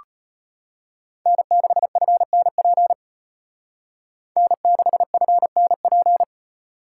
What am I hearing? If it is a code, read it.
Morse code, heard as a keyed tone, about 33 wpm, 700 Hz. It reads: N6FNP N6FNP